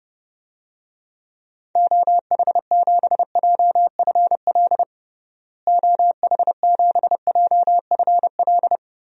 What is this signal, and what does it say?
Morse code, heard as a keyed tone, about 32 wpm, 700 Hz. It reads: OH7JFL OH7JFL
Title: DXCC-100 Call Signs - 6 Characters - Repeated 3x - 30wpm